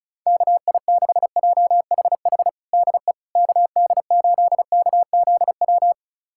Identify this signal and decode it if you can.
Morse code, heard as a keyed tone, about 36 wpm, 700 Hz. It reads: KI6JHH DE KD8KZW
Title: USA Callsign DE Callsign 35wpm